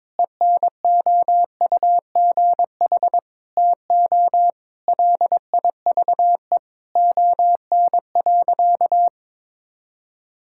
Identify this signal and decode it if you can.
Morse code, heard as a keyed tone, about 22 wpm, 700 Hz. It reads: ENOUGH TO LIVE ON.
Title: Sentences from Top 1000 Words - Review Part 6 - 22wpm